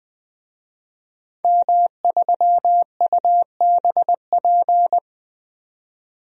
Morse code, heard as a keyed tone, about 20 wpm, 700 Hz. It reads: M3UBP
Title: DXCC-100 Call Signs - 5 Characters - 20wpm